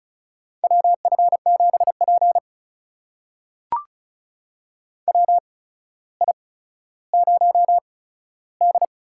Morse code, heard as a keyed tone, about 36 wpm, spaced as for 15 wpm, 700 Hz. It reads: WF7P WI0D
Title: USA Call Signs - 4 Characters - 8x Character Spacing - 35wpm